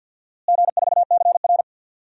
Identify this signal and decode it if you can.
Morse code, heard as a keyed tone, about 49 wpm, 700 Hz. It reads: G4CR
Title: DXCC-100 Call Signs - 4 Characters - Repeated 3x - 50wpm